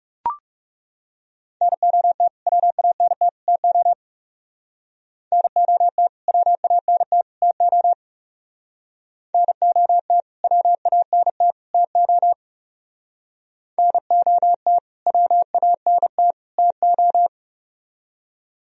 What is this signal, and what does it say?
Morse code, heard as a keyed tone, about 38 wpm, 700 Hz. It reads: NOT WANT TO NOT WANT TO NOT WANT TO NOT WANT TO